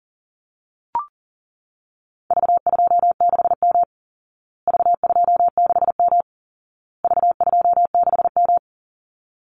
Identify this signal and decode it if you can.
Morse code, heard as a keyed tone, about 40 wpm, 700 Hz. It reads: V26M V26M V26M
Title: Call Signs Repeated 2x - 40wpm